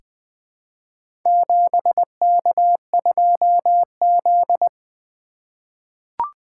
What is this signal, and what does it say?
Morse code, heard as a keyed tone, about 20 wpm, 700 Hz. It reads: 7K2Z
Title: Four Letter-Number Combinations - 20wpm